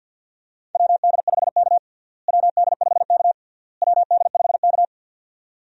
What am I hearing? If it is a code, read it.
Morse code, heard as a keyed tone, about 48 wpm, 700 Hz. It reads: WD5K WD5K WD5K
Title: USA Call Signs - 4 Characters - Repeated 3x - 50wpm